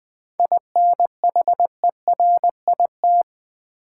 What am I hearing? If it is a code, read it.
Morse code, heard as a keyed tone, about 20 wpm, 700 Hz. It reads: INHERIT
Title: 7 Letter Words - First 3 Letters Given - 20wpm